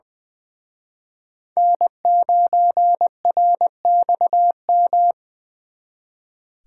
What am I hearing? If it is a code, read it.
Morse code, heard as a keyed tone, about 24 wpm, 700 Hz. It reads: N9RXM
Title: Speed Racing - USA 1x3 Call Signs 20wpm